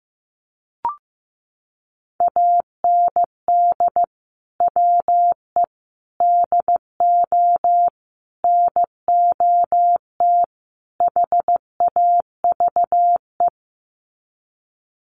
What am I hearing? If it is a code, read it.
Morse code, heard as a keyed tone, about 15 wpm, 700 Hz. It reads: AND WE DO NOT HAVE
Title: Sets of 5 Words Encore 15wpm